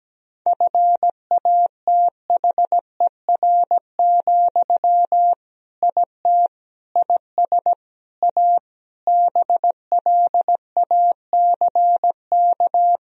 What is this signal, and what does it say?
Morse code, heard as a keyed tone, about 17 wpm, 700 Hz. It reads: FATHER, IT IS A BLACK
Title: Sentences from Top 1000 Words - Review Part 2 - 17wpm